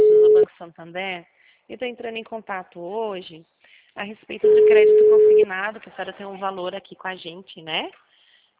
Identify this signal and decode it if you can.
Ringback tone (Europe, 425 Hz), with speech.
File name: out- -2002- - - .5162